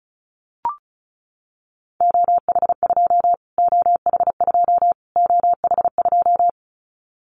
Morse code, heard as a keyed tone, about 34 wpm, 700 Hz. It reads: OH2 OH2 OH2
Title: Callsign Prefixes for Top 25 DXCC - Repeated 3x - 35wpm